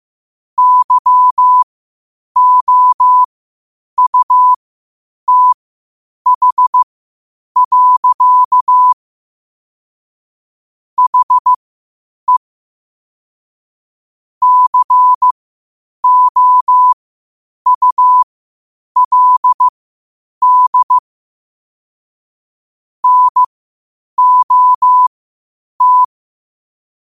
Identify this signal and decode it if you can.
Morse code, heard as a keyed tone, about 15 wpm, spaced as for 8 wpm, 1000 Hz. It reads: YOUTH. HE COULD NOT
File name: qotdfiles / 8wpm